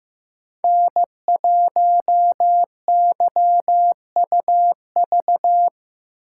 Morse code, heard as a keyed tone, about 15 wpm, 700 Hz. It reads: N1YUV
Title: USA Call Signs - 5 Characters - 2x Character Spacing - 15wpm